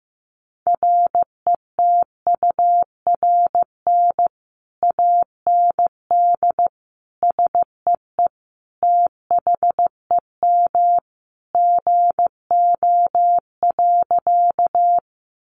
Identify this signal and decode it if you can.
Morse code, heard as a keyed tone, about 14 wpm, 700 Hz. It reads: RETURN AND SEE THEM GO.
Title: Sentences from Top 800 Words 15wpm